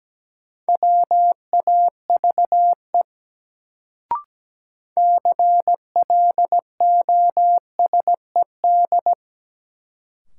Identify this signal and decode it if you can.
Morse code, heard as a keyed tone, about 17 wpm, 700 Hz. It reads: WAVE CLOSED
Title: Sentences from Top 1400 Words - New Word Drill Alt - 17wpm